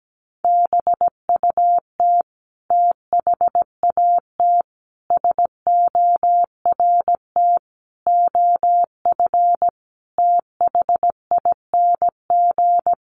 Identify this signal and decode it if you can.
Morse code, heard as a keyed tone, about 17 wpm, 700 Hz. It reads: BUT THAT SORT OF THING
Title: Sentences from Top 1300 Words 17wpm